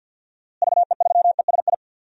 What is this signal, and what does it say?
Morse code, heard as a keyed tone, about 53 wpm, 700 Hz. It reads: VE3ESI
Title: DXCC-100 Call Signs - 6 Characters - Repeated 2x - 50wpm